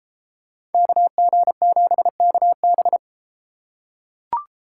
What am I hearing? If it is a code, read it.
Morse code, heard as a keyed tone, about 30 wpm, 700 Hz. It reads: KG7KB